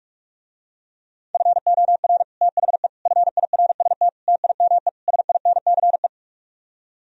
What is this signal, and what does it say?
Morse code, heard as a keyed tone, about 50 wpm, 700 Hz. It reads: FOR THE FIRST TIME SINCE